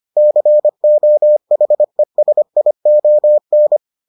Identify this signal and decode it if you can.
Morse code, heard as a keyed tone, about 25 wpm, 600 Hz. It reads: COHESION